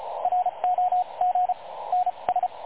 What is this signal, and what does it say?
Morse code, heard as a keyed tone, about 33 wpm, spaced as for 27 wpm, 730 Hz, with radio noise. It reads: ROG NS
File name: dlazaawansowanych / nagraniazpasma / websdr recording 3565.0kHz